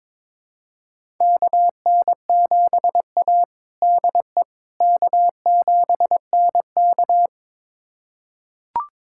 Morse code, heard as a keyed tone, about 22 wpm, 700 Hz. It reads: KN7A DE K7NK
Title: USA Callsign DE Callsign - 4 Characters - 22wpm